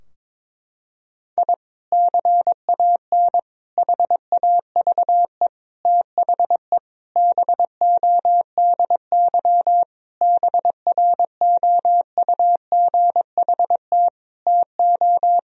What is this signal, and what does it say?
Morse code, heard as a keyed tone, about 22 wpm, 700 Hz. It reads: I CAN HAVE THE BODY BROUGHT TO
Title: Sentences from Top 1000 Words - Review Part 2 - 22wpm